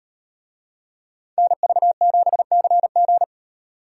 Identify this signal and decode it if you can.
Morse code, heard as a keyed tone, about 38 wpm, 700 Hz. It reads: NV7CG